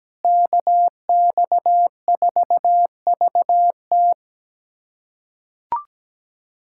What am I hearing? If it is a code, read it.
Morse code, heard as a keyed tone, about 17 wpm, 700 Hz. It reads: KX4VT
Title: USA Callsign DE Callsign 17wpm